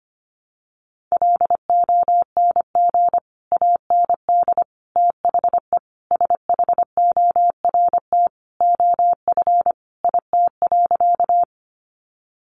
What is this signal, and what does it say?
Morse code, heard as a keyed tone, about 25 wpm, 700 Hz. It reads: LONG AND THE SHORT OF IT.
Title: Sentences from Top 400 Words 25wpm